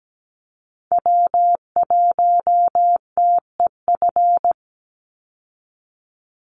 Morse code, heard as a keyed tone, about 17 wpm, 700 Hz. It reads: W1TEF
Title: Call Signs Repeated 2x - 17wpm